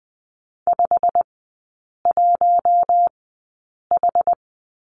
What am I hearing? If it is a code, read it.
Morse code, heard as a keyed tone, about 20 wpm, spaced as for 8 wpm, 700 Hz. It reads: 51H
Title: 093 Lesson 16e - Letter c - Three Characters 20wpm